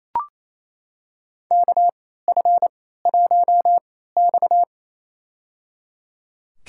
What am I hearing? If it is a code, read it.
Morse code, heard as a keyed tone, about 28 wpm, spaced as for 16 wpm, 700 Hz. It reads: KF1X
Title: USA Call Signs - 4 Characters - 3x Character Spacing - 28wpm